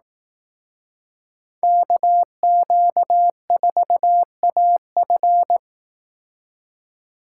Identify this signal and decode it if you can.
Morse code, heard as a keyed tone, about 18 wpm, 700 Hz. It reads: KQ4AF
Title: Speed Racing - USA 2x2 Call Signs 15wpm